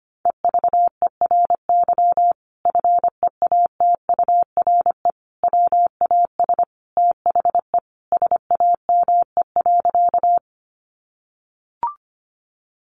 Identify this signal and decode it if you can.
Morse code, heard as a keyed tone, about 25 wpm, 700 Hz. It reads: EVERY FEATURE WAS THE SAME.
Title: Sentences from Top 800 Words 25wpm